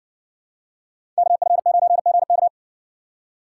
Morse code, heard as a keyed tone, about 61 wpm, 700 Hz. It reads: KF9GK